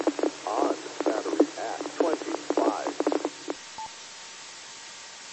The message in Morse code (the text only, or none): none